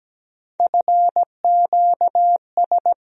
Morse code, heard as a keyed tone, about 17 wpm, 700 Hz. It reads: FQS
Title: USA Call Sign Postfixes - 3 Letters 17wpm